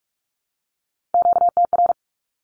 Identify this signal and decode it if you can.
Morse code, heard as a keyed tone, about 44 wpm, 700 Hz. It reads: QTR
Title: Q-Codes Repeated 3x 45wpm